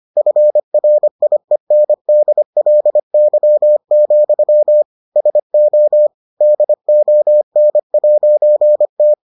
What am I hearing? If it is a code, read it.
Morse code, heard as a keyed tone, about 25 wpm, 600 Hz. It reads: FRIENDLY, SO DON'T